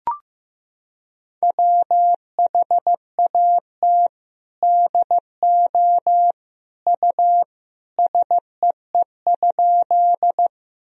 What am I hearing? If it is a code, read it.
Morse code, heard as a keyed tone, about 15 wpm, 700 Hz. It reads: WHAT DO U SEE?
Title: Sentences from Top 100 Words and Abbreviations 15wpm